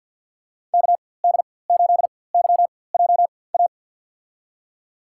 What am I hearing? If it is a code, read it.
Morse code, heard as a keyed tone, about 52 wpm, spaced as for 23 wpm, 700 Hz. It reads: KD8YJA
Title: USA Call Signs - 6 Characters - 4x Character Spacing - 50wpm